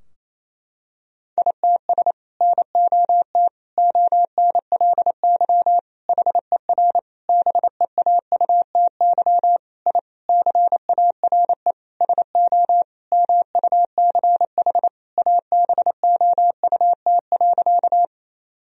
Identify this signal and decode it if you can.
Morse code, heard as a keyed tone, about 28 wpm, 700 Hz. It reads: ITS NOT ONLY HER BEAUTY I CARE SO MUCH ABOUT.
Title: Sentences from Top 1000 Words - Review Part 2 - 28wpm